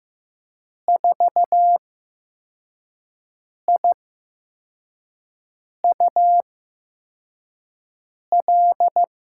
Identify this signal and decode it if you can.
Morse code, heard as a keyed tone, about 15 wpm, spaced as for 4 wpm, 700 Hz. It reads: 4IUL
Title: USA Call Signs - 6 Characters - 8x Character Spacing - 15wpm